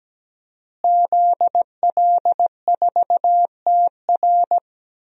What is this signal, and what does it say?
Morse code, heard as a keyed tone, about 17 wpm, 700 Hz. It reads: ZL4TR
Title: DXCC-100 Call Signs - 5 Characters - Repeated 2x - 17wpm